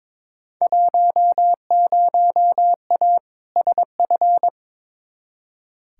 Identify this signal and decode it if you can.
Morse code, heard as a keyed tone, about 22 wpm, 700 Hz. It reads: 10A SF
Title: Field Day Exchanges Encore - 22wpm